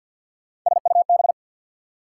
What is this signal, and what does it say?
Morse code, heard as a keyed tone, about 51 wpm, 690 Hz. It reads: SUB
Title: Prefixes and Suffixes 50wpm